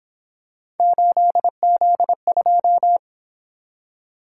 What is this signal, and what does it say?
Morse code, heard as a keyed tone, about 26 wpm, 700 Hz. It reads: 8Z2
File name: Speed-Racing - Three Letter-Number Combinations - 22wpm